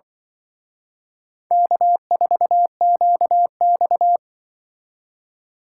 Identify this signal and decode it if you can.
Morse code, heard as a keyed tone, about 25 wpm, 700 Hz. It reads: K4QX
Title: Speed Racing - USA 1x2 Call Signs 20wpm